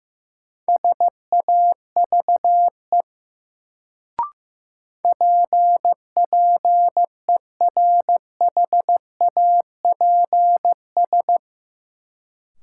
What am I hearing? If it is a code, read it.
Morse code, heard as a keyed tone, about 15 wpm, 700 Hz. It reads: SAVE PPERHAPS
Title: Sentences from Top 1100 Words - New Word Drill Alt - 15wpm